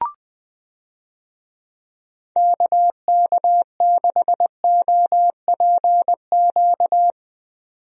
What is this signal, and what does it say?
Morse code, heard as a keyed tone, about 20 wpm, 700 Hz. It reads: KK6OPQ
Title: USA Call Signs - 6 Characters - 20wpm